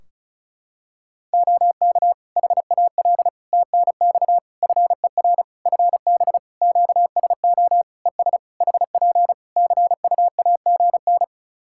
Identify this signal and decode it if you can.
Morse code, heard as a keyed tone, about 36 wpm, 700 Hz. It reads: OK HAL TNX FER FB QSO ES HP CUAGN
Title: Practice QSO - Exchange 4 - Ending - 35wpm